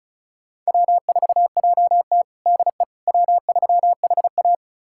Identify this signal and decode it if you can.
Morse code, heard as a keyed tone, about 36 wpm, 700 Hz. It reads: W4JT DE W3HA
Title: USA Callsign DE Callsign - 4 Characters - 35wpm